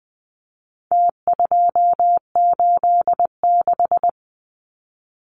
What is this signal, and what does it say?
Morse code, heard as a keyed tone, about 20 wpm, 700 Hz. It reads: T286
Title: Numbers - Contest Style - 4-Digit 20wpm